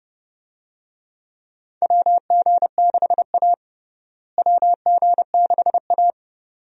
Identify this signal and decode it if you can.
Morse code, heard as a keyed tone, about 32 wpm, 700 Hz. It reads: WG6A WG6A
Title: USA Call Signs - 4 Characters - Repeated 2x - 30wpm